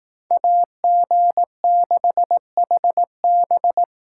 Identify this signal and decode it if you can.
Morse code, heard as a keyed tone, about 18 wpm, 700 Hz. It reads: AG6HB
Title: Speed Racing - USA 2x2 Call Signs 15wpm